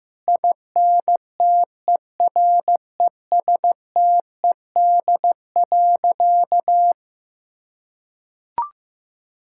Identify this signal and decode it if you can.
Morse code, heard as a keyed tone, about 15 wpm, 700 Hz. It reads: INTERESTED.
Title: Sentences from Top 1000 Words - Review Part 6 - 15wpm